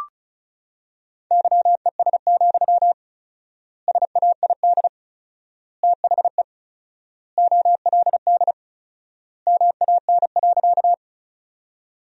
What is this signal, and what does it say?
Morse code, heard as a keyed tone, about 34 wpm, 700 Hz. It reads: YES, SAID THE OLD MAN.
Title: 4x Word-Spacing - Sentences from Top 500 Words - 35wpm